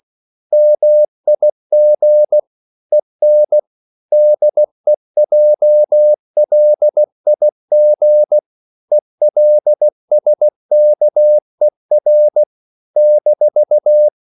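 Morse code, heard as a keyed tone, about 16 wpm, 600 Hz. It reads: MIG EN DEJLIG ELSKER -
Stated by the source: Slow (QRS) Morse code at 16 wpm, from CW med Gnister.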